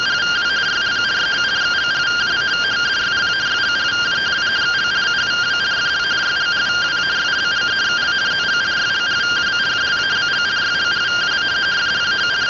Two-tone teleprinter signal, about 170 Hz shift.
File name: findings / axial iodine 10mm 184.gen-19 rtty rx